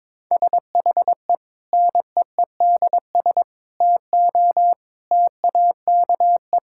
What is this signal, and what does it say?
Morse code, heard as a keyed tone, about 22 wpm, 700 Hz. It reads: SHE NEEDS TO TAKE
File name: practice / 22wpm / Sentences with 4-Word Phrases - 22wpm